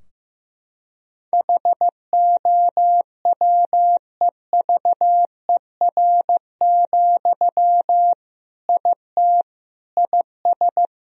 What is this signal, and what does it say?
Morse code, heard as a keyed tone, about 15 wpm, 700 Hz. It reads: HOWEVER, IT IS
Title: Sentences from Top 1000 Words - Review Part 3 - 15wpm